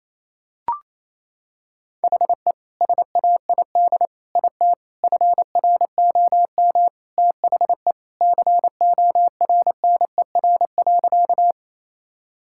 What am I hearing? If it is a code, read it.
Morse code, heard as a keyed tone, about 28 wpm, 700 Hz. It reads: HE SAID IT FROM THE CORNER.
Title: Sentences from Top 1600 Words 28wpm